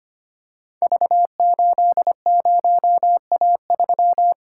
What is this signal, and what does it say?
Morse code, heard as a keyed tone, about 26 wpm, 700 Hz. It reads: V80A3
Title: Code Groups - 5 Letter:Numbers - 25wpm